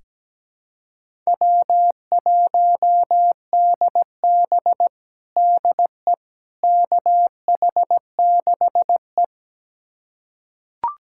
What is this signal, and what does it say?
Morse code, heard as a keyed tone, about 17 wpm, 700 Hz. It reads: W1DB DE KH6E
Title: USA Callsign DE Callsign - 4 Characters - 17wpm